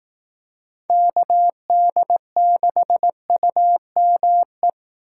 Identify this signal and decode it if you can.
Morse code, heard as a keyed tone, about 19 wpm, 700 Hz. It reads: KD6UME